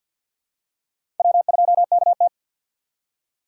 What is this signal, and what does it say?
Morse code, heard as a keyed tone, about 53 wpm, 700 Hz. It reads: W2KT
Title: USA Call Signs - 6x Character Spacing - 50wpm